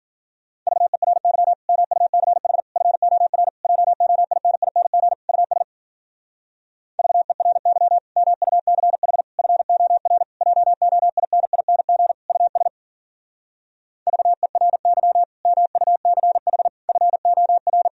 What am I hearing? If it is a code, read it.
Morse code, heard as a keyed tone, about 52 wpm, 700 Hz. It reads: VERY MUCH FOR JOINING US VERY MUCH FOR JOINING US VERY MUCH FOR